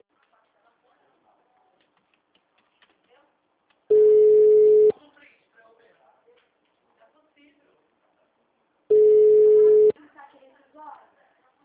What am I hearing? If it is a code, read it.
Ringback tone (Europe, 425 Hz), with speech.
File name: out- -2021- - - .39526